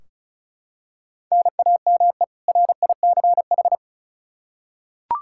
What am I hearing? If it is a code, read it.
Morse code, heard as a keyed tone, about 36 wpm, 700 Hz. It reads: NAME RICH
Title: K1USN SST QSO Practice 35wpm